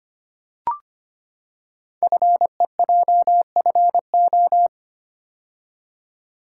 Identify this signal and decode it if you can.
Morse code, heard as a keyed tone, about 25 wpm, 690 Hz. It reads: FEJFO
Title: Code Groups - 5 Letters - 25wpm